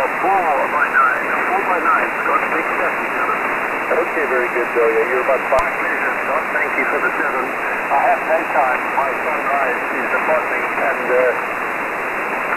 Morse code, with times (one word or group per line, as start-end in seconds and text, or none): none